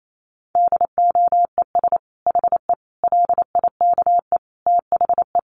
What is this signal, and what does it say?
Morse code, heard as a keyed tone, about 28 wpm, 700 Hz. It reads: DOES HE LIKE THE
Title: Sentences from Top 600 Words 28wpm